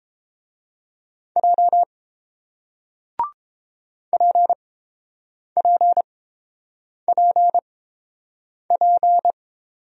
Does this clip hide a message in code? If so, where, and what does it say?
Yes: Morse code from 1.4 to 9.3 s, heard as a keyed tone, about 30 wpm, spaced as for 7 wpm, 690 Hz. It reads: J PPPP